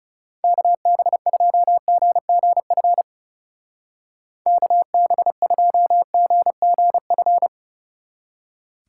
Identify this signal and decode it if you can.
Morse code, heard as a keyed tone, about 37 wpm, 700 Hz. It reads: KB2GGF KB2GGF
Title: Speed Racing - USA 2x3 Call Signs 30wpm